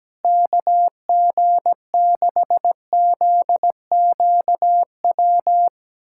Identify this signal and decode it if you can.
Morse code, heard as a keyed tone, about 17 wpm, 700 Hz. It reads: KG6ZQW
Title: USA Callsign DE Callsign - 6 Characters - 17wpm